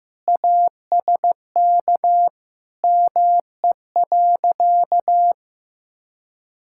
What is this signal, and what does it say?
Morse code, heard as a keyed tone, about 15 wpm, 700 Hz. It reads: ASK ME.
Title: Sentences from Top 1000 Words - Review Part 5 - 15wpm